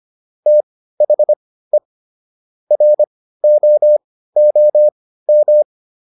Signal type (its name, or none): Morse code